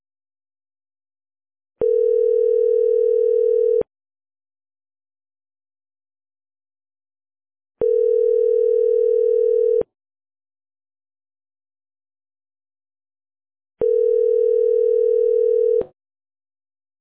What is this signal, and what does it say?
Ringback tone (North America), 440 + 480 Hz.